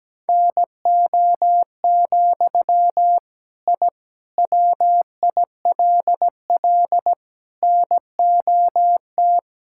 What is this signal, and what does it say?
Morse code, heard as a keyed tone, about 17 wpm, 700 Hz. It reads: NO, I WILL NOT
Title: Sentences from Top 1000 Words - Review Part 5 - 17wpm